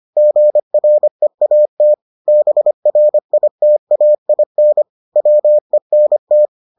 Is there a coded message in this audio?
Yes: Morse code from 0.2 to 6.5 s, heard as a keyed tone, about 25 wpm, 600 Hz. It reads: GREAT BRITAIN WENT